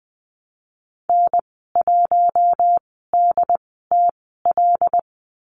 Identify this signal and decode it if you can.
Morse code, heard as a keyed tone, about 20 wpm, spaced as for 15 wpm, 700 Hz. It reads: N1DTL
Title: 122 Lesson 20g - Number 3 - Call Signs 20wpm